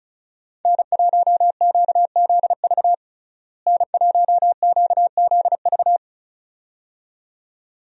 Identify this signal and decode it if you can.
Morse code, heard as a keyed tone, about 36 wpm, 700 Hz. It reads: N1QZV N1QZV
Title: USA Call Signs - 5 Characters - Repeated 2x - 35wpm